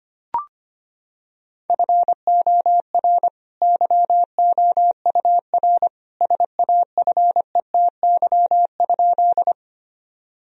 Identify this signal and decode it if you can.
Morse code, heard as a keyed tone, about 25 wpm, 700 Hz. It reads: FOR YOUR SAFETY?
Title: Sentences from Top 1000 Words - Review Part 4 - 25wpm